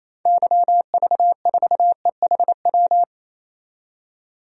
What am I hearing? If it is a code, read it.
Morse code, heard as a keyed tone, about 28 wpm, 700 Hz. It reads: YV4EHW
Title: DXCC-100 Call Signs - 6 Characters - 28wpm